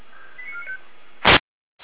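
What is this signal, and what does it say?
Five-tone selective call (ZVEI), heard as quick runs of five tones, tones of about 90 ms. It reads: 89495
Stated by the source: ZVEI five-tone selective call received on SDR, feuerwehr-orlamuende.org fire brigade audio server.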